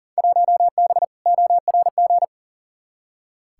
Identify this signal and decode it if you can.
Morse code, heard as a keyed tone, about 40 wpm, 700 Hz. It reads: 1B ORG
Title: Field Day Exchanges Encore - 40wpm